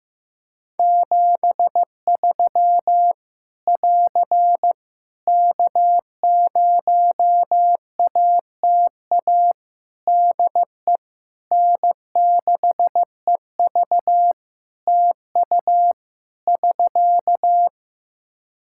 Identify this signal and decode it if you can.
Morse code, heard as a keyed tone, about 15 wpm, 700 Hz. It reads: 73 <AR> K0ATA DE N6EV TU <SK>
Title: Practice QSO - Exchange 4 - Ending - 15wpm